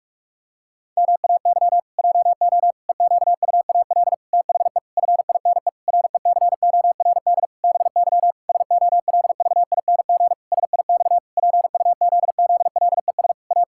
Morse code, heard as a keyed tone, about 46 wpm, 700 Hz. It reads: MAY JO EQUAL THE FINE RECORD BY SOLVING SIX PUZZLES A
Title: Pangram Challenge v2 - 45wpm